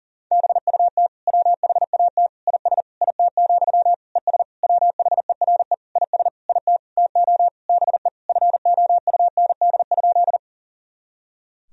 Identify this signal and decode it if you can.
Morse code, heard as a keyed tone, about 40 wpm, 700 Hz. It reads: BUT WHAT IS IT, ES WHERE IS IT TO BE FOUND?
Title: Sentences from Top 300 Words and Abbreviations 40wpm